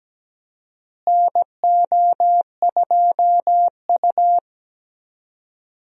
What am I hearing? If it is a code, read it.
Morse code, heard as a keyed tone, about 17 wpm, 700 Hz. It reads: NO2U
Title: USA Call Signs - 4 Characters - Repeated 3x - 17wpm